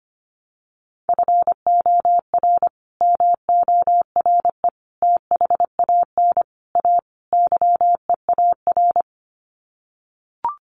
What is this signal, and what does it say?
Morse code, heard as a keyed tone, about 25 wpm, 700 Hz. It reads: FOR MORE THAN A YEAR
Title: Sets of 5 Words 25wpm